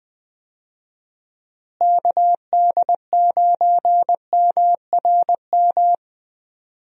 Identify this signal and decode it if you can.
Morse code, heard as a keyed tone, about 20 wpm, 700 Hz. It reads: KD9MRM
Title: USA Call Signs - 6 Characters - Repeated 2x - 20wpm